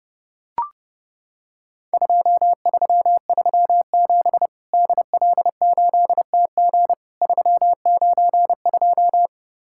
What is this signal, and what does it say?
Morse code, heard as a keyed tone, about 30 wpm, 700 Hz. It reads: 2337 DL8TG 392
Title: Worked All Europe DX Contest - QTC Exchange - 30wpm